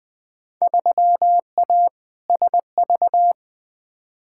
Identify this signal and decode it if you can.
Morse code, heard as a keyed tone, about 20 wpm, 700 Hz. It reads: 3A SV
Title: Field Day Exchanges - 20wpm